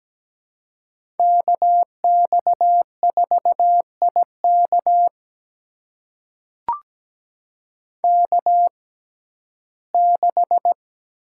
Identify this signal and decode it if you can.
Morse code, heard as a keyed tone, about 17 wpm, spaced as for 12 wpm, 700 Hz. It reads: KX4IK K6
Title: USA Call Signs - 5 Characters - 6x Character Spacing - 17wpm